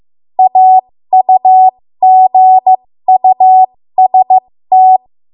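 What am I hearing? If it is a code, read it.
Morse code, heard as a keyed tone, about 15 wpm, spaced as for 13 wpm, 750 Hz. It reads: AUGUST
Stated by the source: 13 wpm Morse code practice archive transmission.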